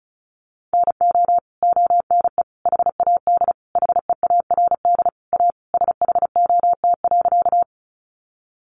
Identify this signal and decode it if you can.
Morse code, heard as a keyed tone, about 34 wpm, 700 Hz. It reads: NO ONE HAD HEARD A SHOT.
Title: Sentences from Top 1100 Words 35wpm